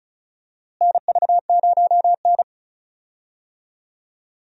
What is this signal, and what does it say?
Morse code, heard as a keyed tone, about 36 wpm, 700 Hz. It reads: NV0N
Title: USA Call Signs - 4 Characters - Repeated 3x - 35wpm